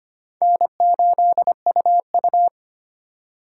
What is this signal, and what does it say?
Morse code, heard as a keyed tone, about 25 wpm, 700 Hz. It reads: N8UU
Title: USA Call Signs - 4 Characters - 3x Character Spacing - 25wpm